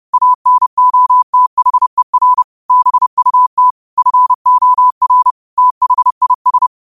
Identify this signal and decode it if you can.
Morse code, heard as a keyed tone, about 30 wpm, 1000 Hz. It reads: ANOTHER BUT FOR THIS
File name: qotdfiles / 30wpm